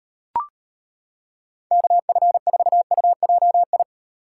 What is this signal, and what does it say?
Morse code, heard as a keyed tone, about 30 wpm, 700 Hz. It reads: KF4UJI
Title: Speed Racing - USA 2x3 Call Signs 25wpm